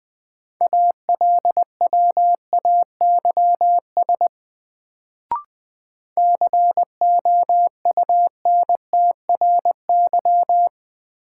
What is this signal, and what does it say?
Morse code, heard as a keyed tone, about 20 wpm, 700 Hz. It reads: ALWAYS COUNTRY
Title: Sentences from Top 300 Words - New Word Drill - 20wpm